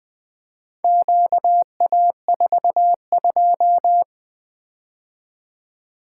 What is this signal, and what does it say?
Morse code, heard as a keyed tone, about 20 wpm, 700 Hz. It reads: QA42
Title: Four Letter-Number Combinations - 20wpm